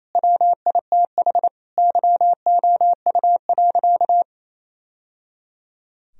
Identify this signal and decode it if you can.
Morse code, heard as a keyed tone, about 28 wpm, 700 Hz. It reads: WITH YOU.
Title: Sentences from Top 1000 Words - Review Part 4 - 28wpm